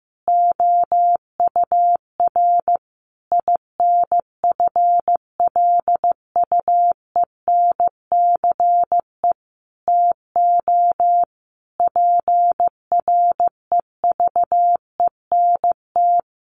Morse code, heard as a keyed tone, about 15 wpm, 700 Hz. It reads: OUR INFLUENCE TO PREVENT
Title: Sentences from Top 1100 Words 15wpm